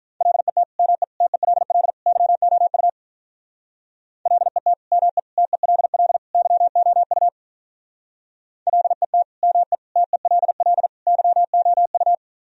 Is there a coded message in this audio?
Yes: Morse code from 0.2 to 12.1 s, heard as a keyed tone, about 46 wpm, 700 Hz. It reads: LET ME TELL YOU LET ME TELL YOU LET ME TELL YOU